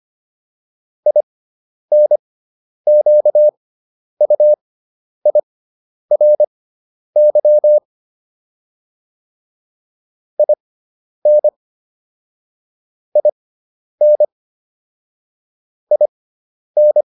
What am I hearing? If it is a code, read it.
Morse code, heard as a keyed tone, about 25 wpm, spaced as for 10 wpm, 600 Hz. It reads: INQUIRY IN IN IN